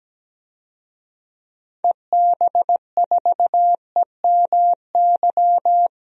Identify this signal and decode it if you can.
Morse code, heard as a keyed tone, about 17 wpm, 700 Hz. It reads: EB4EMY
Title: DXCC-100 Call Signs - 6 Characters - 17wpm